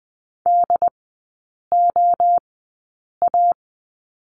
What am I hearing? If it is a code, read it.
Morse code, heard as a keyed tone, about 20 wpm, spaced as for 8 wpm, 700 Hz. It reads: DOA
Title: 133 Lesson 22e - Question Mark - Three Characters 20wpm